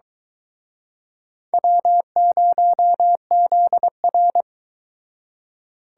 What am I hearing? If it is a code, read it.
Morse code, heard as a keyed tone, about 23 wpm, 700 Hz. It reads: W0ZR